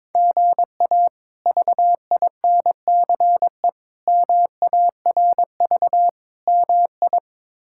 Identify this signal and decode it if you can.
Morse code, heard as a keyed tone, about 22 wpm, 700 Hz. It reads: GA VINCE MARV MI
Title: K1USN SST QSO Practice - 22wpm